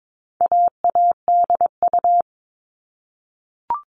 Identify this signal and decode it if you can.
Morse code, heard as a keyed tone, about 22 wpm, 700 Hz. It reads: AADU
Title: All Cut Numbers - 4-Digit 22wpm